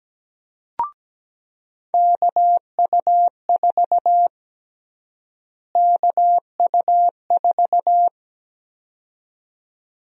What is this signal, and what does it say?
Morse code, heard as a keyed tone, about 17 wpm, 700 Hz. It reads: KU4 KU4
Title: USA Call Sign Prefixes - Repeated 2x - 17wpm